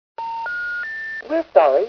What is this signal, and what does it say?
Special information tone, with speech.